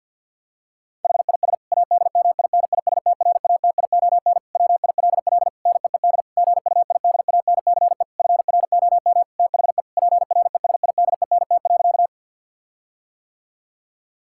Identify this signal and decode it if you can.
Morse code, heard as a keyed tone, about 50 wpm, 700 Hz. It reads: HIS ADMINISTRATION WILL NEED GUIDANCE FROM THE PRESIDENT.
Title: Sentences from Top 1600 Words 50wpm